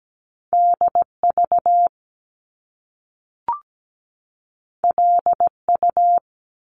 Morse code, heard as a keyed tone, about 17 wpm, 700 Hz. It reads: DV LU
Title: Top Two Letter Combinations in Words 17wpm